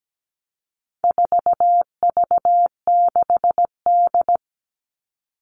Morse code, heard as a keyed tone, about 17 wpm, 700 Hz. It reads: 4V6D
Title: All Cut Numbers - 4-Digit 17wpm